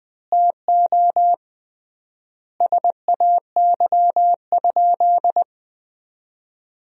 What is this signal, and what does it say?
Morse code, heard as a keyed tone, about 21 wpm, 700 Hz. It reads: TO SAY?
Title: 3x Word-Spacing - Sentences from Top 200 Words - 20wpm